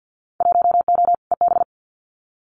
Morse code, heard as a keyed tone, about 50 wpm, 700 Hz. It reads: 1O EB
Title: Winter Field Day Exchanges 50wpm